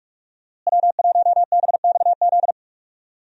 Morse code, heard as a keyed tone, about 46 wpm, 700 Hz. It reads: W1BXZ
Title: USA Call Signs - 5 Characters - Repeated 2x - 45wpm